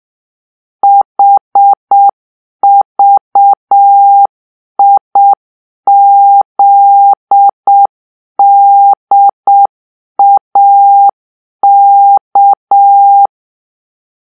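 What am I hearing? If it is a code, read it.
Morse code, heard as a keyed tone, about 7 wpm, 800 Hz. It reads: HVIZDAK
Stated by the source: Morse-code cipher made by a player team for the Kachna puzzle game.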